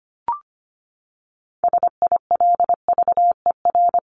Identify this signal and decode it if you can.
Morse code, heard as a keyed tone, about 25 wpm, 700 Hz. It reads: SILVER
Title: Top 1751-2000 Words 25wpm